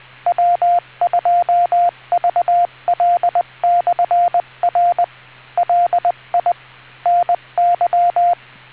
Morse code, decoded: W2VL/R LI NY